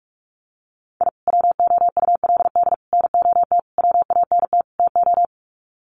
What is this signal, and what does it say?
Morse code, heard as a keyed tone, about 44 wpm, 700 Hz. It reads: I WOULD NOT WANT TO
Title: Sets of 5 Words 45wpm